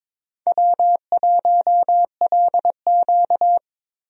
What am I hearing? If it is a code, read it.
Morse code, heard as a keyed tone, about 21 wpm, 700 Hz. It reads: W1LQ
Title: USA Call Signs - 4 Characters - Repeated 3x - 22wpm